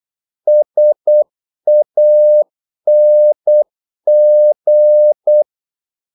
Morse code, heard as a keyed tone, about 8 wpm, 600 Hz. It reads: SANG
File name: Korte ord DK 8wpm